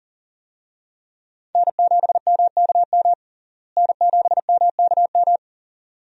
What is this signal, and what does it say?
Morse code, heard as a keyed tone, about 40 wpm, 700 Hz. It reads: N7MKM N7MKM
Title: USA Call Signs - 5 Characters - Repeated 2x - 40wpm